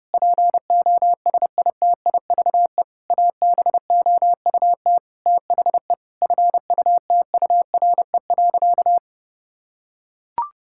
Morse code, heard as a keyed tone, about 32 wpm, 700 Hz. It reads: POSITIVE ABOUT THE FUTURE.
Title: Sentences from Top 1000 Words - Review Part 5 - 30wpm